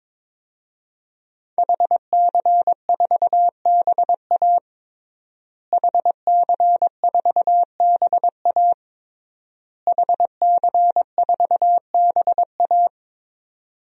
Morse code, heard as a keyed tone, about 22 wpm, 700 Hz. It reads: HC4BA HC4BA HC4BA
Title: DXCC-100 Call Signs - 5 Characters - Repeated 3x - 22wpm